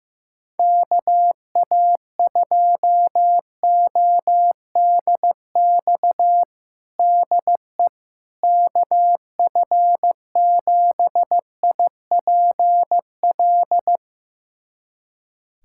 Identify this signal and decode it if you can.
Morse code, heard as a keyed tone, about 15 wpm, 700 Hz. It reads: KA2ODX DE KF7IPL
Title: USA Callsign DE Callsign - 6 Characters - 15wpm